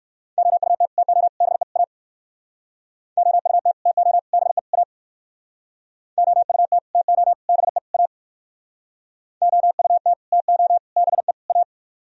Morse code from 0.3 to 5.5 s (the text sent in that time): OUT TO BE A OUT TO BE A